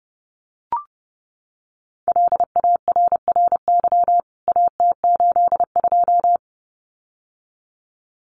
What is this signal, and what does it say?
Morse code, heard as a keyed tone, about 30 wpm, 700 Hz. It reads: LARRY AT82
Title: CWT Exchange 30wpm